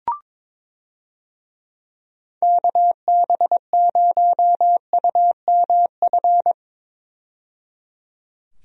Morse code, heard as a keyed tone, about 22 wpm, 700 Hz. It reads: KB0UMF
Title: USA Call Signs - 6 Characters - 22wpm